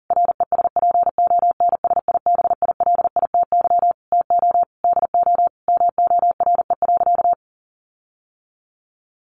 Morse code, heard as a keyed tone, about 40 wpm, 700 Hz. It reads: RESPONSIBILITY TO DO MORE.
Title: Sentences from Top 1400 Words 40wpm